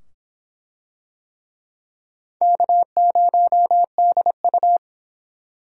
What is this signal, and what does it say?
Morse code, heard as a keyed tone, about 24 wpm, 700 Hz. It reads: K0DU